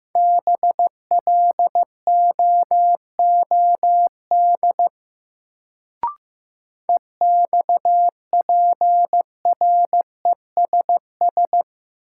Morse code, heard as a keyed tone, about 15 wpm, 700 Hz. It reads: BLOOD EXPRESS
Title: Sentences from Top 1100 Words - New Word Drill Alt - 15wpm